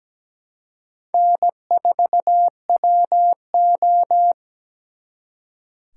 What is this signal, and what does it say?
Morse code, heard as a keyed tone, about 20 wpm, 700 Hz. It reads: N4WO
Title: Speed Racing - USA 1x2 Call Signs 17wpm